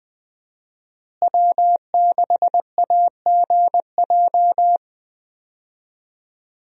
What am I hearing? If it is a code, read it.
Morse code, heard as a keyed tone, about 20 wpm, 700 Hz. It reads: W6AGJ
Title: USA Call Signs - 5 Characters - Repeated 3x - 20wpm